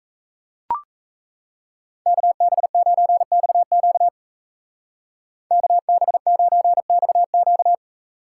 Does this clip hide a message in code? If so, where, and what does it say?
Yes: Morse code from 2.1 to 7.7 s, heard as a keyed tone, about 36 wpm, 700 Hz. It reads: KB9XQ KB9XQ